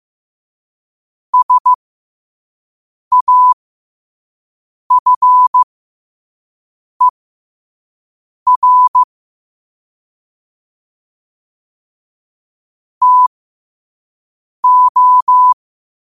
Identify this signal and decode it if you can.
Morse code, heard as a keyed tone, about 15 wpm, spaced as for 5 wpm, 1000 Hz. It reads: SAFER TO